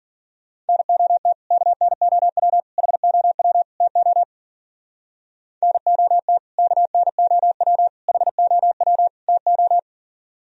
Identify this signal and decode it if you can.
Morse code, heard as a keyed tone, about 50 wpm, 700 Hz. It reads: NOT KNOW HOW TO NOT KNOW HOW TO
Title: Speed-Racing - Sets of 4 Words 40wpm